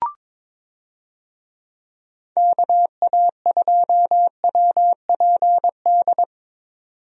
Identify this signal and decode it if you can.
Morse code, heard as a keyed tone, about 22 wpm, 700 Hz. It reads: KA2WPD
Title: USA Call Signs - 6 Characters - 22wpm